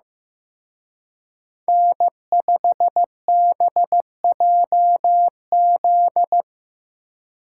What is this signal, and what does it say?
Morse code, heard as a keyed tone, about 18 wpm, 700 Hz. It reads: N5BJZ
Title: Speed Racing - USA 1x3 Call Signs 15wpm